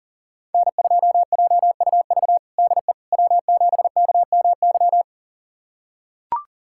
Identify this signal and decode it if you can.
Morse code, heard as a keyed tone, about 37 wpm, 700 Hz. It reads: N2JUV DE W7KMY
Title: USA Callsign DE Callsign - 5 Characters - 40wpm